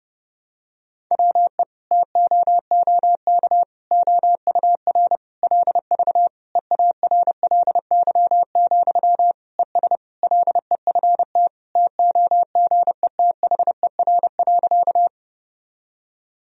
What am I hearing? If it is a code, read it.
Morse code, heard as a keyed tone, about 30 wpm, 700 Hz. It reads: WE TOOK OUR LV EARLY, ES LEFT TOGETHER.
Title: Sentences from Top 400 Words and Abbreviations 30wpm